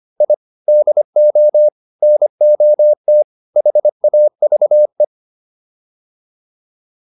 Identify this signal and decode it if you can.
Morse code, heard as a keyed tone, about 25 wpm, 600 Hz. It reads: I DO NOT HAVE